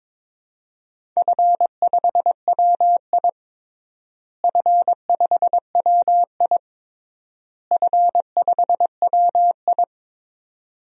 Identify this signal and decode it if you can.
Morse code, heard as a keyed tone, about 22 wpm, 700 Hz. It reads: F5WI F5WI F5WI
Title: DXCC-100 Call Signs - 4 Characters - Repeated 3x - 22wpm